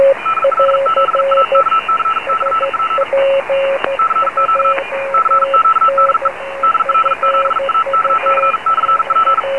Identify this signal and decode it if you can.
Morse code, heard as a keyed tone, about 44 wpm, 1320 Hz, with radio noise. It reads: DL7AKC/P CQ CQ DE DL7AKC/P